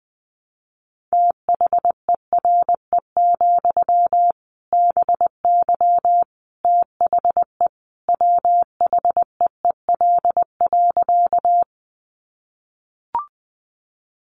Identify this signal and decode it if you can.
Morse code, heard as a keyed tone, about 20 wpm, 700 Hz. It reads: THERE, BY THE WHEEL.
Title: Sentences from Top 500 Words 20wpm